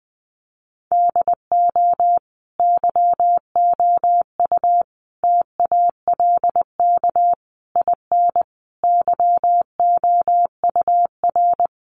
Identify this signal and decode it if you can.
Morse code, heard as a keyed tone, about 20 wpm, 700 Hz. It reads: DO YOU TALK IN YOUR
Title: Sentences from Top 500 Words 20wpm